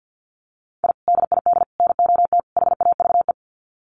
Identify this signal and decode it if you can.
Morse code, heard as a keyed tone, about 50 wpm, 700 Hz. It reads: I DID NOT HAVE
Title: Sets of 4 Words 50wpm